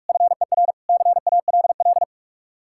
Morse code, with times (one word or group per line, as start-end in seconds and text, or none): 0.1-0.7 FER
0.9-2.0 CALL